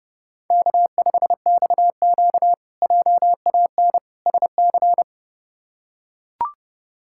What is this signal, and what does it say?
Morse code, heard as a keyed tone, about 30 wpm, 700 Hz. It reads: K5XQ JAN SC
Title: K1USN SST QSO Practice - 30wpm